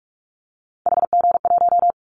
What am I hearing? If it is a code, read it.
Morse code, heard as a keyed tone, about 46 wpm, 700 Hz. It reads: HG1
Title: Callsign Prefixes for Top 50 DXCC - 45wpm